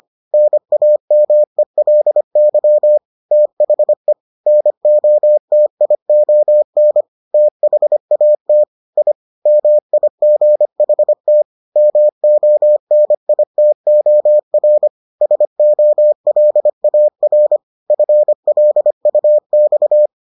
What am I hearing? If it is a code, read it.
Morse code, heard as a keyed tone, about 25 wpm, 600 Hz. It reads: NAMELY THE NOTION THAT I MIGHT MONITOR SOLAR FLUX